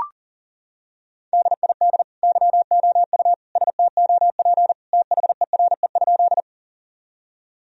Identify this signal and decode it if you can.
Morse code, heard as a keyed tone, about 40 wpm, 700 Hz. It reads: DID YOU STOP THERE?
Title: Sentences from Top 1000 Words - Review Part 1 - 40wpm